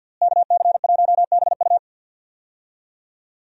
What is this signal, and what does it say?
Morse code, heard as a keyed tone, about 49 wpm, 700 Hz. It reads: KC1BU
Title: USA Call Signs - 5 Characters - Repeated 3x - 50wpm